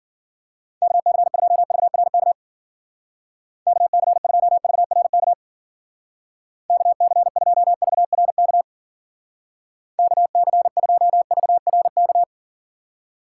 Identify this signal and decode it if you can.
Morse code, heard as a keyed tone, about 48 wpm, 700 Hz. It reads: KC2VRK KC2VRK KC2VRK KC2VRK
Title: Speed Racing - USA 2x3 Call Signs 40wpm